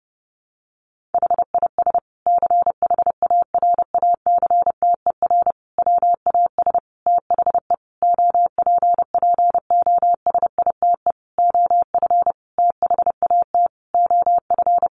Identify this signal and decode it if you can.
Morse code, heard as a keyed tone, about 30 wpm, 700 Hz. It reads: HIS CHARACTER WAS THE OPPOSITE OF THAT OF
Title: Sentences from Top 1300 Words 30wpm